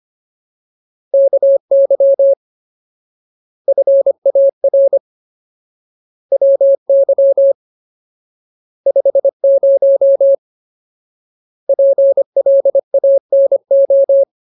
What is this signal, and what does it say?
Morse code, heard as a keyed tone, about 25 wpm, 550 Hz. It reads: KY FAR WY 50 PLANO
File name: Ultimate Word List / 25WPM Ultimate